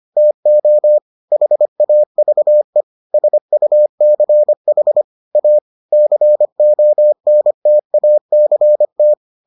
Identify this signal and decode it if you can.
Morse code, heard as a keyed tone, about 25 wpm, 600 Hz. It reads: TO HAVE SUCH A CONTACT